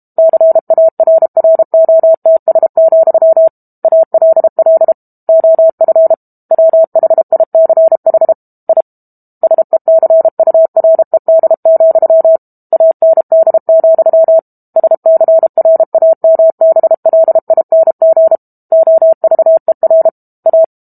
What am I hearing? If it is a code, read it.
Morse code, heard as a keyed tone, about 32 wpm, 650 Hz. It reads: CARROTS, ALL OF WHICH I SECURED, AND, SCRAMBLING OVER A
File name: Chapter 6 - 32 WPM